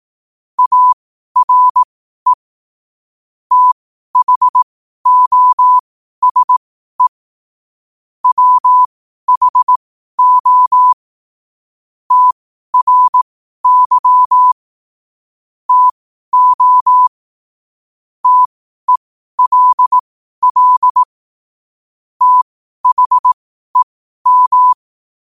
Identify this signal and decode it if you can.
Morse code, heard as a keyed tone, about 18 wpm, spaced as for 12 wpm, 1000 Hz. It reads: ARE THOSE WHO TRY TO TELL THEM
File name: qotdfiles / 12wpm